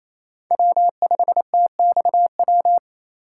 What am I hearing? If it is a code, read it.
Morse code, heard as a keyed tone, about 27 wpm, 700 Hz. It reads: W5TXW
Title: USA Call Signs - 5 Characters - 4x Character Spacing - 28wpm